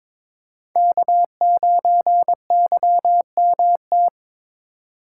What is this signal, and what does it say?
Morse code, heard as a keyed tone, about 21 wpm, 700 Hz. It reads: K9YMT
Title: USA Call Signs - 5 Characters - 2x Character Spacing - 22wpm